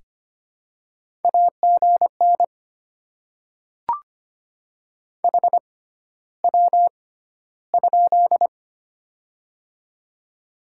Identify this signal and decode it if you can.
Morse code, heard as a keyed tone, about 25 wpm, spaced as for 13 wpm, 700 Hz. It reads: AGN HW?
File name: Top 20 QSO Elements - 6x Character Spacing - 25wpm